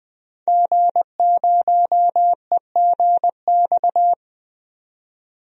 Morse code, heard as a keyed tone, about 20 wpm, 700 Hz. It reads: G0EGX
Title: DXCC-100 Call Signs - 5 Characters - 20wpm